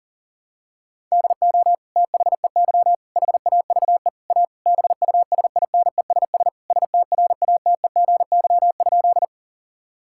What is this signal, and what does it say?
Morse code, heard as a keyed tone, about 40 wpm, 700 Hz. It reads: DO THEY HAVE A BUSINESS STRATEGY?
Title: Sentences from Top 1000 Words - Review Part 5 - 40wpm